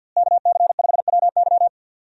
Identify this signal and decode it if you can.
Morse code, heard as a keyed tone, about 52 wpm, 700 Hz. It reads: KC5WY
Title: USA Call Signs - 5 Characters - Repeated 3x - 50wpm